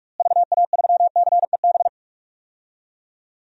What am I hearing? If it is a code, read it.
Morse code, heard as a keyed tone, about 41 wpm, 700 Hz. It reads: VA3CEB
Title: DXCC-100 Call Signs - 6 Characters - Repeated 2x - 45wpm